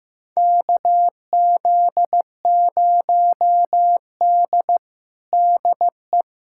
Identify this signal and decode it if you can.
Morse code, heard as a keyed tone, about 15 wpm, 700 Hz. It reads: KZ0D DE
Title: USA Callsign DE Callsign - 4 Characters - 15wpm